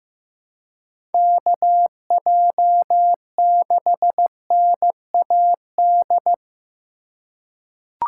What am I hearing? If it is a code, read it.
Morse code, heard as a keyed tone, about 15 wpm, 700 Hz. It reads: KJ6NAD
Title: USA Call Signs - 6 Characters - 2x Character Spacing - 15wpm